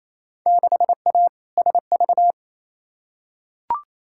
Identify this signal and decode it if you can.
Morse code, heard as a keyed tone, about 28 wpm, 700 Hz. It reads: 6A SV
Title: Field Day Exchanges Encore - 28wpm